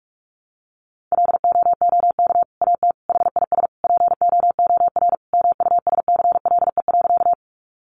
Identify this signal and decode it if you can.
Morse code, heard as a keyed tone, about 44 wpm, 700 Hz. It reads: LOOK AT HIS POOR MUSCLE.
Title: Sentences from Top 1300 Words 45wpm